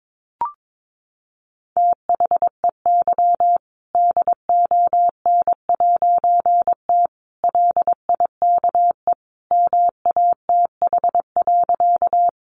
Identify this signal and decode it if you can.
Morse code, heard as a keyed tone, about 22 wpm, 700 Hz. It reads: THEY DON'T LIKE MATH.
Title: Easy Phrases 22wpm